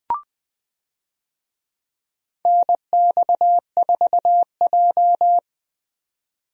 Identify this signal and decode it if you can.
Morse code, heard as a keyed tone, about 20 wpm, 700 Hz. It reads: NX4J
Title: USA Call Signs - 4 Characters - 20wpm